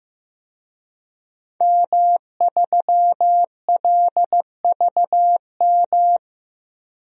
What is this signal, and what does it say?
Morse code, heard as a keyed tone, about 15 wpm, 700 Hz. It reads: M3LVM
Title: DXCC-100 Call Signs - 5 Characters - Repeated 2x - 15wpm